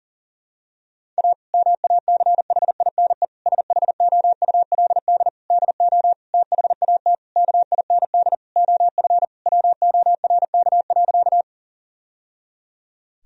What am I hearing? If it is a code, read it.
Morse code, heard as a keyed tone, about 40 wpm, 700 Hz. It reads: A MACHINE SHOULD DO THAT KIND OF WORK.
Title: Sentences from Top 1000 Words - Review Part 2 - 40wpm